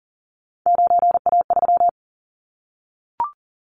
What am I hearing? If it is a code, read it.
Morse code, heard as a keyed tone, about 39 wpm, 700 Hz. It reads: 9A3
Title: Callsign Prefixes for Top 25 DXCC - 40wpm